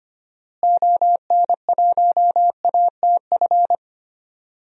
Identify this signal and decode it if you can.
Morse code, heard as a keyed tone, about 25 wpm, 700 Hz. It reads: ON1ATF
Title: DXCC-100 Call Signs - 6 Characters - Repeated 3x - 25wpm